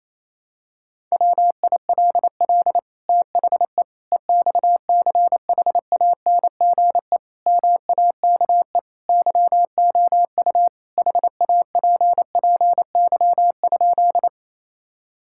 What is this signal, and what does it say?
Morse code, heard as a keyed tone, about 28 wpm, 690 Hz. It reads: WILL THE EXCHANGE MAKE YOU HAPPY?
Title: Sentences from Top 1000 Words - Review Part 6 - 28wpm